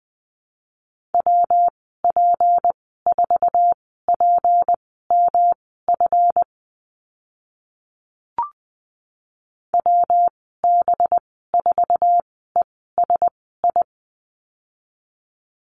Morse code, heard as a keyed tone, about 20 wpm, spaced as for 14 wpm, 700 Hz. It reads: WP4PMF WB4ESI
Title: 193 Lesson 31g - Letter v - Call Signs 20wpm